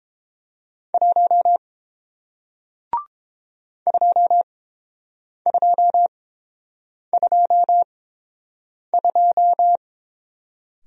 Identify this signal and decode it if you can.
Morse code, heard as a keyed tone, about 29 wpm, spaced as for 7 wpm, 700 Hz. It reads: 1 2222